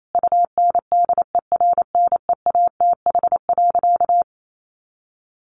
Morse code, heard as a keyed tone, about 28 wpm, 700 Hz. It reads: UNDERNEATH.
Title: A Sound Of Thunder 28wpm